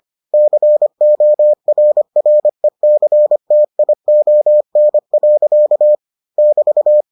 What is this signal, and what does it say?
Morse code, heard as a keyed tone, about 25 wpm, 600 Hz. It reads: CORRECTION. <BT>